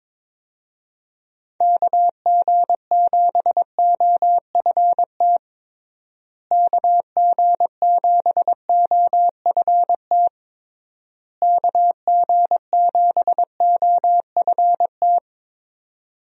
Morse code, heard as a keyed tone, about 21 wpm, 700 Hz. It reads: KG7OFT KG7OFT KG7OFT
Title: USA Call Signs - 6 Characters - Repeated 3x - 22wpm